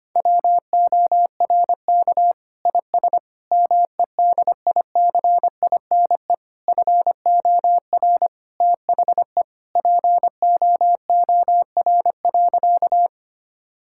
Morse code, heard as a keyed tone, about 25 wpm, 700 Hz. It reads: WORK IS MEDICINE FOR THE POOR.
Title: Sentences from Top 1000 Words - Review Part 6 - 25wpm